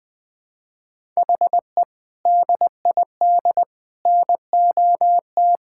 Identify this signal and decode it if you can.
Morse code, heard as a keyed tone, about 20 wpm, 700 Hz. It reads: HE DID NOT
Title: Sentences with 5-Word Phrases - 20wpm